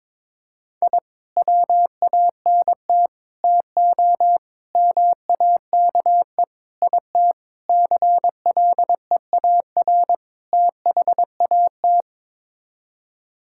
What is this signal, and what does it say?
Morse code, heard as a keyed tone, about 21 wpm, 700 Hz. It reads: I WANT TO MAKE IT CLEAR THAT
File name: practice / 22wpm / Sets of 7 Words - 22wpm